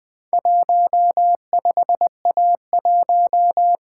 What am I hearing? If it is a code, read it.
Morse code, heard as a keyed tone, about 20 wpm, 700 Hz. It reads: 15A1
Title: Four Letter-Number Combinations - 20wpm